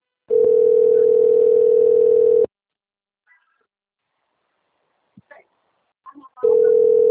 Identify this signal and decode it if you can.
Ringback tone (North America), 440 + 480 Hz.
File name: rg-1600-unknown- - - .8934